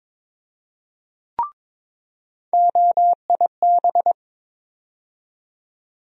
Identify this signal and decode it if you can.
Morse code, heard as a keyed tone, about 22 wpm, 700 Hz. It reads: OIB
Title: USA Call Sign Postfixes - 3 Letters 22wpm